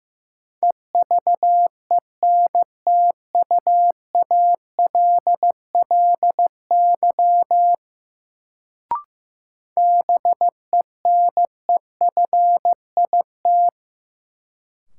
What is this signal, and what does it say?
Morse code, heard as a keyed tone, about 15 wpm, 700 Hz. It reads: EVENTUALLY BENEFIT
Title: Sentences from Top 1000 Words - Review Part 6 - New Word Drill - 15wpm